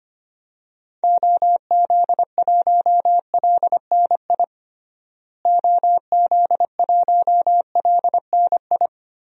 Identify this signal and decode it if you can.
Morse code, heard as a keyed tone, about 25 wpm, 700 Hz. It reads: OZ1LNI OZ1LNI
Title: DXCC-100 Call Signs - 6 Characters - Repeated 3x - 25wpm